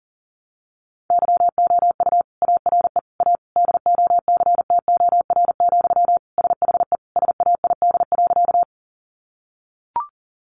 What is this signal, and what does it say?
Morse code, heard as a keyed tone, about 40 wpm, 700 Hz. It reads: YOU ARE A DOCTOR, SHE SAID.
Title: Sentences from Top 1500 Words 40wpm